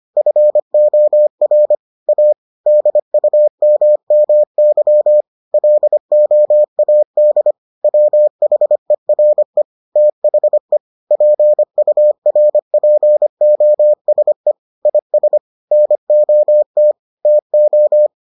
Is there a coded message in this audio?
Yes: Morse code from 0.2 to 18.1 s, heard as a keyed tone, about 25 wpm, 600 Hz. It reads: FOR A DUMMY LOAD WHERE THE PURPOSE IS NOT TO